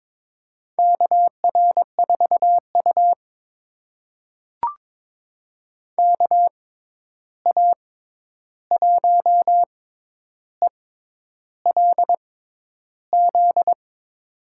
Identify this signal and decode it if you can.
Morse code, heard as a keyed tone, about 22 wpm, spaced as for 10 wpm, 700 Hz. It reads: KR4U KA1ELZ
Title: USA Call Signs - 6x Character Spacing - 22wpm